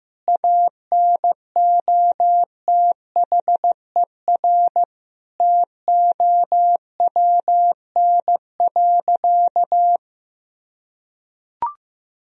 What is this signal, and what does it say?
Morse code, heard as a keyed tone, about 15 wpm, 700 Hz. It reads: ANOTHER TOWN.
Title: Sentences from Top 1000 Words - Review Part 2 - 15wpm